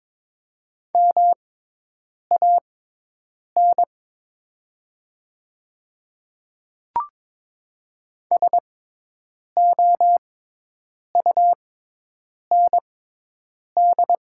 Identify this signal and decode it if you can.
Morse code, heard as a keyed tone, about 22 wpm, spaced as for 8 wpm, 700 Hz. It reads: MAN SOUND
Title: 6x Character Spacing - Top 200 Words - 22wpm